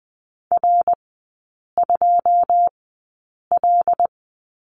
Morse code, heard as a keyed tone, about 20 wpm, spaced as for 8 wpm, 700 Hz. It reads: R2L
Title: 107 Lesson 18e - Letter m - Three Characters 20wpm